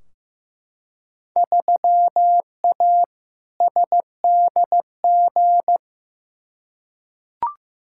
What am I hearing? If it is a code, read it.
Morse code, heard as a keyed tone, about 15 wpm, 700 Hz. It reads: 3A SDG
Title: Field Day Exchanges - 15wpm